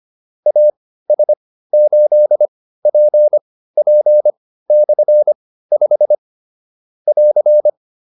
Morse code, heard as a keyed tone, about 25 wpm, spaced as for 15 wpm, 600 Hz. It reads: AS8PP/5 <AR>
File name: QSO 302 15